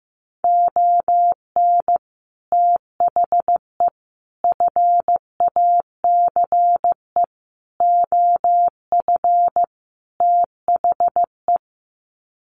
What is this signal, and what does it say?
Morse code, heard as a keyed tone, about 15 wpm, 700 Hz. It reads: ON THE FACE OF THE
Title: Sets of 5 Words Encore2 15wpm